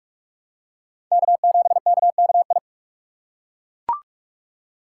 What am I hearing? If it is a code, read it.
Morse code, heard as a keyed tone, about 41 wpm, 700 Hz. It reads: K7KKI